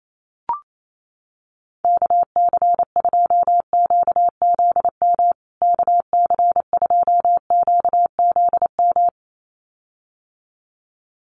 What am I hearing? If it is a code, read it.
Morse code, heard as a keyed tone, about 28 wpm, 700 Hz. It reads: KC2QZM KC2QZM
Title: USA Call Signs Repeated 2x 28wpm